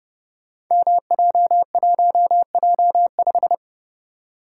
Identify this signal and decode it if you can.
Morse code, heard as a keyed tone, about 29 wpm, 700 Hz. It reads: MJ1J5
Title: Code Groups - 5 Letter:Numbers - 30wpm